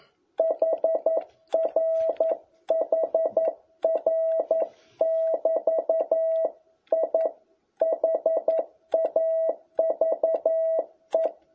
Morse code, heard as a keyed tone, about 11 wpm, 650 Hz. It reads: HR HR<BT> I HAVE
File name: jh8cbh / 2021cw